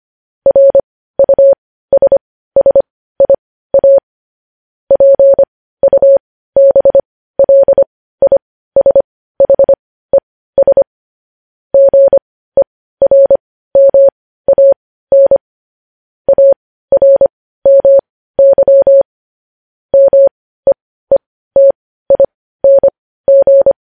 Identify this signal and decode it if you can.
Morse code, heard as a keyed tone, about 25 wpm, spaced as for 15 wpm, 550 Hz. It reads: RUSSIA PUBLISHES GERMAN ARMY MEETING
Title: News Headlines at 15